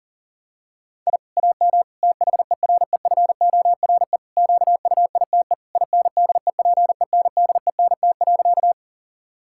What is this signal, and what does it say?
Morse code, heard as a keyed tone, about 40 wpm, 700 Hz. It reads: I AM THEREFORE QUITE INDEPENDENT.
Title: Sentences from Top 1000 Words - Review Part 6 - 40wpm